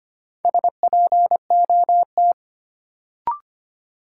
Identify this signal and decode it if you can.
Morse code, heard as a keyed tone, about 25 wpm, 700 Hz. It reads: SPOT
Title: Sentences from Top 1100 Words - New Word Drill - 25wpm